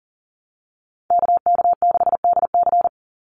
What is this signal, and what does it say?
Morse code, heard as a keyed tone, about 40 wpm, 700 Hz. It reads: KK6DC
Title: USA Call Signs 40wpm